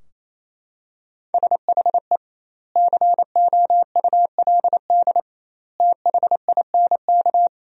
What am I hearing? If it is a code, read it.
Morse code, heard as a keyed tone, about 28 wpm, 700 Hz. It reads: SHE COULD THINK
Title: 2x Word-Spacing - Sentences from Top 500 Words - 28wpm